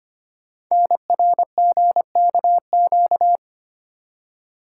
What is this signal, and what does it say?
Morse code, heard as a keyed tone, about 25 wpm, 700 Hz. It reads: NRGKQ
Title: Code Groups - 5 Letters - 25wpm